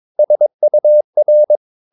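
Morse code, heard as a keyed tone, about 22 wpm, 600 Hz. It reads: SUR
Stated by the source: Morse practice of short Danish words at 22 wpm, from CW med Gnister.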